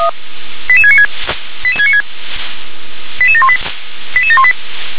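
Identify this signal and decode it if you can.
Five-tone selective call (ZVEI), heard as quick runs of five tones, tones of about 70 ms. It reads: 88696 88696 88718 88718
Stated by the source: Five-tone ZVEI selective call received on SDR.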